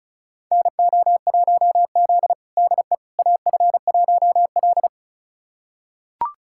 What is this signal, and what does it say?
Morse code, heard as a keyed tone, about 34 wpm, 690 Hz. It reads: NO1Z DE AF1L
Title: USA Callsign DE Callsign - 4 Characters - 35wpm